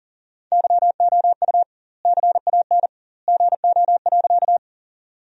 Morse code, heard as a keyed tone, about 37 wpm, 700 Hz. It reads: YOU CAN GO.
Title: 2x Word-Spacing - Sentences from Top 100 Words - 40wpm